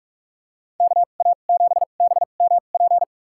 Morse code, heard as a keyed tone, about 46 wpm, spaced as for 32 wpm, 700 Hz. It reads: KA7BMP
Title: USA Call Signs - 6 Characters - 2x Character Spacing - 45wpm